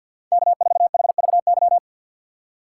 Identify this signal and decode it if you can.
Morse code, heard as a keyed tone, about 52 wpm, 700 Hz. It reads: K4HVY
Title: USA Call Signs - 5 Characters - 3x Character Spacing - 50wpm